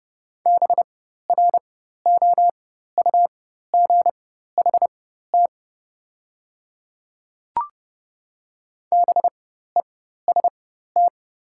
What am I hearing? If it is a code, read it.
Morse code, heard as a keyed tone, about 30 wpm, spaced as for 14 wpm, 700 Hz. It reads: BROUGHT BEST
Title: 4x Character Spacing - Top 500 Words - 30wpm